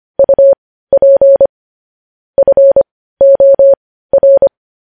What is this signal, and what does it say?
Morse code, heard as a keyed tone, about 25 wpm, spaced as for 15 wpm, 550 Hz. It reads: UP FOR
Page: news headlines in morse code at 15 wpm